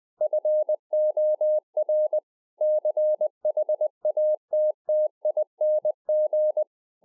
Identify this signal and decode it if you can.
Morse code, heard as a keyed tone, about 20 wpm, 610 Hz. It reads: FOR CHATTING
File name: 2011-04-mp3 / A SMALL TRANCEIVER 3 20WPM